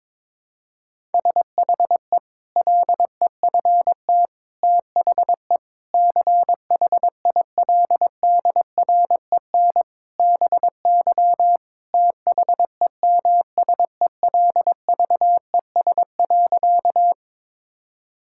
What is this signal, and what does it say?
Morse code, heard as a keyed tone, about 22 wpm, 700 Hz. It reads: SHE LEFT THE CHILDREN BY THEMSELVES.
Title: Sentences from Top 1000 Words - Review Part 5 - 22wpm